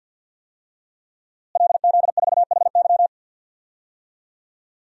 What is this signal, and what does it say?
Morse code, heard as a keyed tone, about 47 wpm, 700 Hz. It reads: LZ4HY
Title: DXCC-100 Call Signs - 5 Characters - Repeated 3x - 50wpm